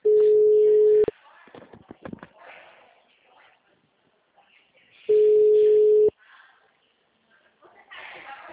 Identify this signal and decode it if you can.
Ringback tone (Europe, 425 Hz), with speech.